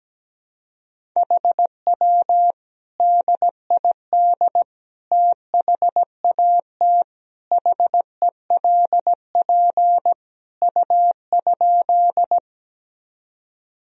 Morse code, heard as a keyed tone, about 17 wpm, 700 Hz. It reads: HW DID THAT HELP U?
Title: Sentences from Top 200 Words and Abbreviations 17wpm